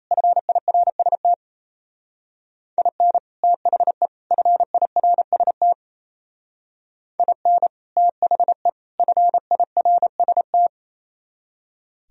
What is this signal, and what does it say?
Morse code, heard as a keyed tone, about 34 wpm, 700 Hz. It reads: FIRST IN THE FIRST IN THE FIRST
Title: Speed-Racing - Sets of 3 Words 28wpm